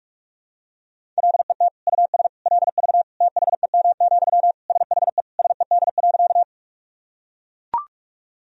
Morse code, heard as a keyed tone, about 41 wpm, 700 Hz. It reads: LET US LV THEM, SHE SED.
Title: Sentences from Top 400 Words and Abbreviations 45wpm